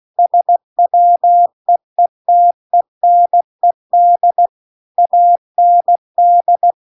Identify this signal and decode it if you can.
Morse code, heard as a keyed tone, about 16 wpm, 700 Hz. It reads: SWEETENED AND
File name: CW-The-Big-Twoheaded-River-2-16WPM